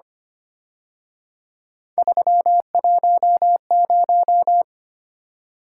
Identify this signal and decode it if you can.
Morse code, heard as a keyed tone, about 30 wpm, 700 Hz. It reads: 310
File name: Speed-Racing - Three Letter-Number Combinations - 25wpm